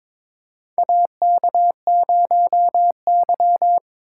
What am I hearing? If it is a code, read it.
Morse code, heard as a keyed tone, about 22 wpm, 700 Hz. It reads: AK0Y
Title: USA Call Signs - 4 Characters - 6x Character Spacing - 22wpm